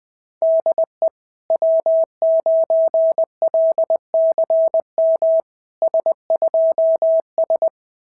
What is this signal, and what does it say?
Morse code, heard as a keyed tone, about 21 wpm, 650 Hz. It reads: DE W9LCM S2S
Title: SOTA QSO Practice 20wpm